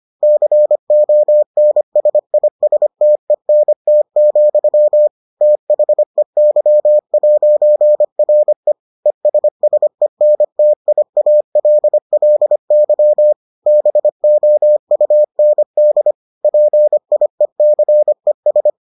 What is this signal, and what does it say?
Morse code, heard as a keyed tone, about 25 wpm, 600 Hz. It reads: CONSISTENT, THEY'RE ESSENTIALLY BOUND PIECES